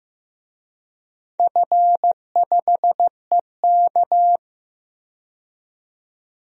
Morse code, heard as a keyed tone, about 15 wpm, 700 Hz. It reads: F5EK
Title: DXCC-100 Call Signs - 4 Characters - 15wpm